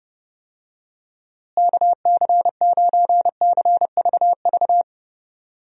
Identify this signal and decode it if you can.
Morse code, heard as a keyed tone, about 30 wpm, 700 Hz. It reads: KC9CVV
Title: USA Call Signs - 6 Characters - Repeated 3x - 30wpm